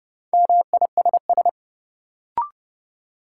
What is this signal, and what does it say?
Morse code, heard as a keyed tone, about 30 wpm, 700 Hz. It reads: MISS
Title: Sentences from Top 500 Words - New Word Drill - 30wpm